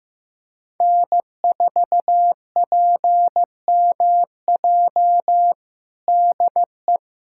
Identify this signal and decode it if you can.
Morse code, heard as a keyed tone, about 15 wpm, 700 Hz. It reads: N4PMJ DE
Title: USA Callsign DE Callsign - 5 Characters - 15wpm